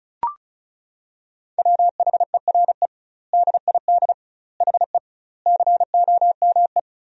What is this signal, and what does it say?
Morse code, heard as a keyed tone, about 36 wpm, 700 Hz. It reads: WHERE DID HE COME
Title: 2x Word-Spacing - Sentences from Top 200 Words - 35wpm